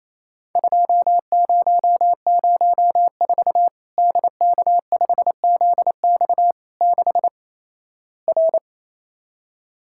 Morse code, heard as a keyed tone, about 27 wpm, 700 Hz. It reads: 2004 DK5ZX 6 R
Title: Worked All Europe DX Contest - QTC Exchange - 28wpm